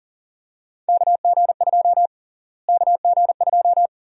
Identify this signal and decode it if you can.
Morse code, heard as a keyed tone, about 40 wpm, 700 Hz. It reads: KG2 KG2
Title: USA Call Sign Prefixes - Repeated 2x - 40wpm